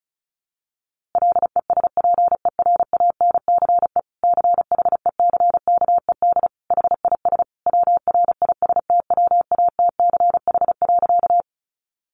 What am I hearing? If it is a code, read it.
Morse code, heard as a keyed tone, about 34 wpm, 700 Hz. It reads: LESPERANCE CHECKED HIS WRISTWATCH.
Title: A Sound Of Thunder 35wpm